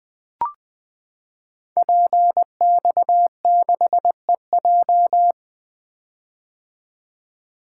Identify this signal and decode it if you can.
Morse code, heard as a keyed tone, about 20 wpm, 700 Hz. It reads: PX6EJ
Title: Code Groups - 5 Letter:Numbers - 20wpm